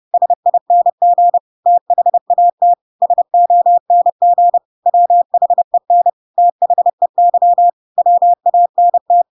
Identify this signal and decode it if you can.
Morse code, heard as a keyed tone, about 30 wpm, 700 Hz. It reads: SING THAT SONG WHEN THEY WANT
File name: Pinocchio / Pinocchio ch8 30 30 ik1hge